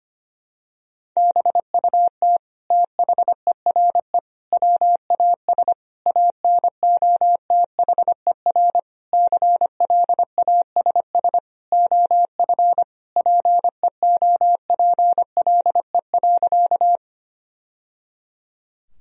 Morse code, heard as a keyed tone, about 25 wpm, 700 Hz. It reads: BUT THERE WAS ANOTHER CLASS OF PEOPLE.
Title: Sentences from Top 1000 Words - Review Part 2 - 25wpm